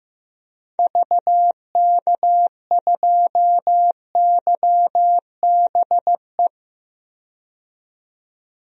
Morse code, heard as a keyed tone, about 15 wpm, 700 Hz. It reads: VK2YBE
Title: DXCC-100 Call Signs - 6 Characters - 15wpm